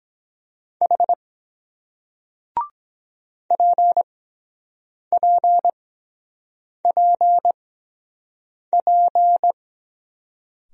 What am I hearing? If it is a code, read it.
Morse code, heard as a keyed tone, about 23 wpm, spaced as for 7 wpm, 700 Hz. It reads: H PPPP